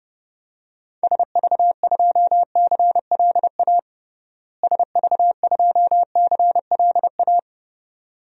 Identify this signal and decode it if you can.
Morse code, heard as a keyed tone, about 32 wpm, 700 Hz. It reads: SV2CLA SV2CLA
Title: DXCC-100 Call Signs - 6 Characters - Repeated 2x - 30wpm